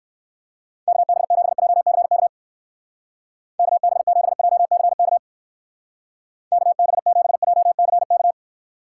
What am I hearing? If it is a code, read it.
Morse code, heard as a keyed tone, about 58 wpm, 700 Hz. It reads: KB7JCK KB7JCK KB7JCK